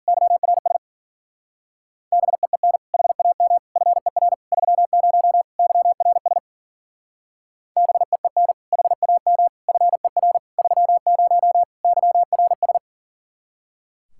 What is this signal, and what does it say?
Morse code, heard as a keyed tone, about 50 wpm, 700 Hz. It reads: YRS BEEN HAM FER 30 YRS BEEN HAM FER 30 YRS
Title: Speed-Racing Common QSO Phrases Encore - 40wpm